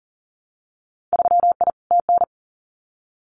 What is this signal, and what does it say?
Morse code, heard as a keyed tone, about 40 wpm, 700 Hz. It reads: 3I TN
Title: Winter Field Day Exchanges 40wpm